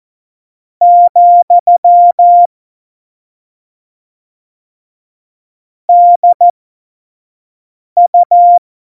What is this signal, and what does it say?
Morse code, heard as a keyed tone, about 14 wpm, spaced as for 5 wpm, 700 Hz. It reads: , DU
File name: Morsekurs / Witze / Witz 14. August 2018, 25 F70